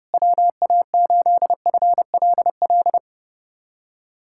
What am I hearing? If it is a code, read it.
Morse code, heard as a keyed tone, about 32 wpm, 700 Hz. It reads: WA8FLL
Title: USA Callsign DE Callsign - 6 Characters - 30wpm